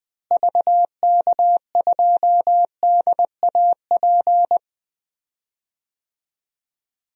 Morse code, heard as a keyed tone, about 20 wpm, 700 Hz. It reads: VK2DAP
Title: DXCC-100 Call Signs - 6 Characters - Repeated 3x - 20wpm